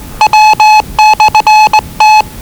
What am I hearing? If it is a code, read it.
Morse code, heard as a keyed tone, about 20 wpm, 860 Hz. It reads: W/T
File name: WellingtonRadioZlwCloseDownOnHfAndMf / 01-Zlw 4277khz 417.5 qrt until 0925z